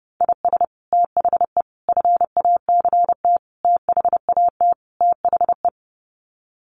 Morse code, heard as a keyed tone, about 30 wpm, 700 Hz. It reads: IS THE FACT THAT THE
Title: Sets of 5 Words Encore2 30wpm